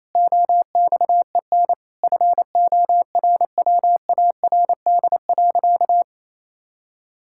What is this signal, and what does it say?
Morse code, heard as a keyed tone, about 28 wpm, 700 Hz. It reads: OXEN FORWARD.
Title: Pangram Challenge v2 - 28wpm